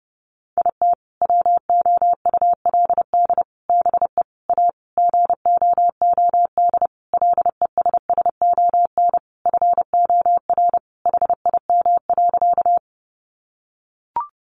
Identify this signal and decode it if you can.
Morse code, heard as a keyed tone, about 30 wpm, 700 Hz. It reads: IT WOULD BE A GOOD LESSON FOR HIM.
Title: Sentences from Top 1200 Words 30wpm